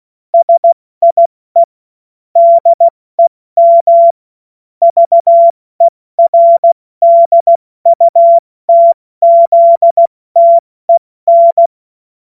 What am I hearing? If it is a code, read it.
Morse code, heard as a keyed tone, about 16 wpm, 680 Hz. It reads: SIE DEM VERDUTZTEN
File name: Morsekurs / Witze / Witz 01. August 2023 70 F80